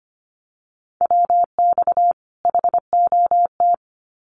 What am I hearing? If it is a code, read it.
Morse code, heard as a keyed tone, about 25 wpm, 700 Hz. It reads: WX HOT
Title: Common QSO Phrases 25wpm